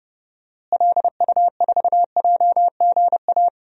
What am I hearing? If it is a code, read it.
Morse code, heard as a keyed tone, about 31 wpm, 700 Hz. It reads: LU4JGA
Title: DXCC-100 Call Signs - 6 Characters - Repeated 3x - 30wpm